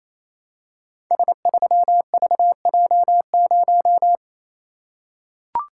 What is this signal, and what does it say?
Morse code, heard as a keyed tone, about 27 wpm, 700 Hz. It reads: S3VJ0
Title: Code Groups - 5 Letter:Numbers - 28wpm